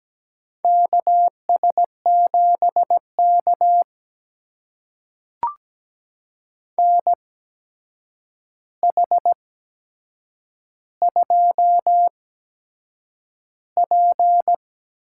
Morse code, heard as a keyed tone, about 17 wpm, spaced as for 7 wpm, 700 Hz. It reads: KS7K NH2P
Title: USA Call Signs - 8x Character Spacing - 17wpm